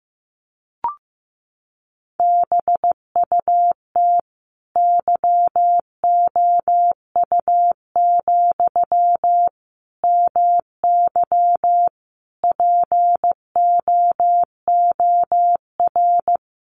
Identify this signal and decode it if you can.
Morse code, heard as a keyed tone, about 15 wpm, 700 Hz. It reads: BUT YOU, MY POOR
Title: Sentences from Top 1000 Words 15wpm